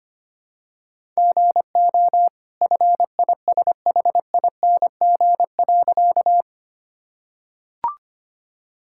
Morse code, heard as a keyed tone, about 25 wpm, 700 Hz. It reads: GO FISHING.
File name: Easy Sentences with Hints - 25wpm